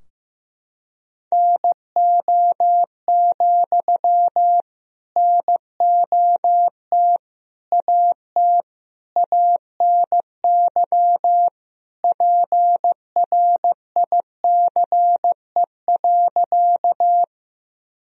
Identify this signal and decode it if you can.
Morse code, heard as a keyed tone, about 15 wpm, 700 Hz. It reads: NO, NOT AT ANY PRICE.
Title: Sentences from Top 1000 Words - Review Part 4 - 15wpm